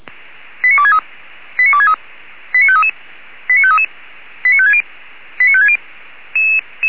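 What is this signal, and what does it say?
Five-tone selective call (ZVEI), heard as quick runs of five tones, tones of about 80 ms. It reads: 89263 89263 89533 89533 89560 89560
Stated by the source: ZVEI five-tone selective call received on an SDR.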